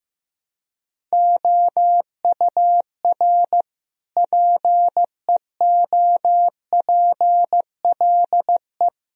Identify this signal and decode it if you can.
Morse code, heard as a keyed tone, about 16 wpm, 700 Hz. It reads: OUR PEOPLE
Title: Sentences from Top 1000 Words - Review Part 4 - 15wpm